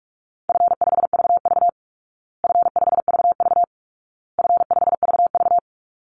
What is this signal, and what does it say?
Morse code, heard as a keyed tone, about 44 wpm, 700 Hz. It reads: F5VV F5VV F5VV
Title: Easy Call Signs Repeated 3x - 45wpm